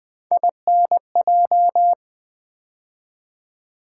Morse code, heard as a keyed tone, about 20 wpm, 700 Hz. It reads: INJ
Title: Build A Word - 3 Letters At A Time - 9-Letter Words - 20wpm